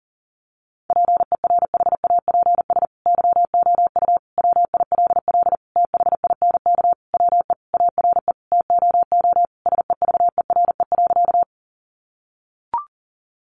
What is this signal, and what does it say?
Morse code, heard as a keyed tone, about 40 wpm, 700 Hz. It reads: PERHAPS YOU WILL THINK WE ARE TOO SEVERE.
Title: Sentences from Top 1400 Words 40wpm